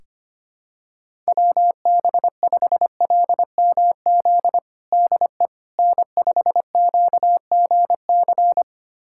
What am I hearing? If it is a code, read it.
Morse code, heard as a keyed tone, about 25 wpm, 700 Hz. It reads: WB5LMZ DE N5QGC
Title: USA Callsign DE Callsign 25wpm